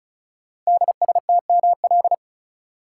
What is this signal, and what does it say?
Morse code, heard as a keyed tone, about 36 wpm, 690 Hz. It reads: DSTML
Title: Code Groups - 5 Letters - 35wpm